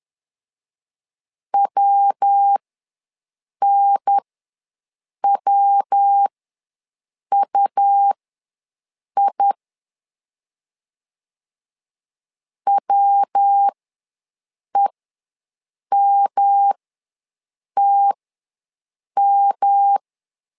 Morse code, heard as a keyed tone, about 10 wpm, spaced as for 6 wpm, 800 Hz. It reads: WNWUI WEMTM
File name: LEZIONI / 3 MWRO / 3 MW ws 10wpm 2213